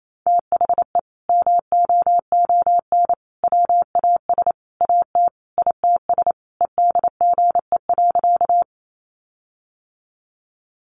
Morse code, heard as a keyed tone, about 28 wpm, 700 Hz. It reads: THE MOON WAS AT ITS EDGE.
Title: Sentences from Top 1300 Words 28wpm